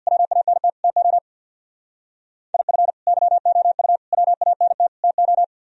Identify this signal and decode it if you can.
Morse code, heard as a keyed tone, about 62 wpm, 690 Hz. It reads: WANT TO IF YOU WANT TO
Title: Speed-Racing - Sets of 4 Words 50wpm